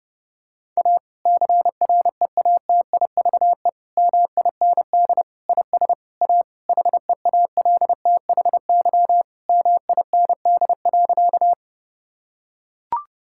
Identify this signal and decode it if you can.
Morse code, heard as a keyed tone, about 32 wpm, 700 Hz. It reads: A CREATIVE MIND IS A HEALTHY MIND.
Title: Sentences from Top 1000 Words - Review Part 5 - 30wpm